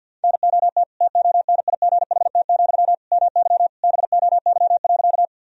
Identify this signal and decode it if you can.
Morse code, heard as a keyed tone, about 51 wpm, 700 Hz. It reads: NOT TONIGHT, MY BOY.
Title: Sentences from Top 1000 Words - Review Part 3 - 50wpm